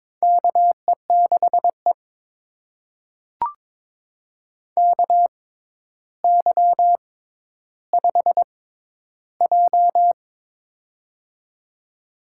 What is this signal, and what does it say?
Morse code, heard as a keyed tone, about 22 wpm, spaced as for 11 wpm, 700 Hz. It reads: KE6E KY5J
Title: USA Call Signs - 4 Characters - 6x Character Spacing - 22wpm